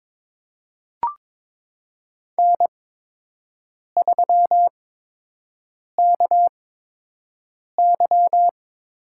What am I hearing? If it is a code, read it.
Morse code, heard as a keyed tone, about 21 wpm, spaced as for 6 wpm, 700 Hz. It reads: N3KY
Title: USA Call Signs - 8x Character Spacing - 22wpm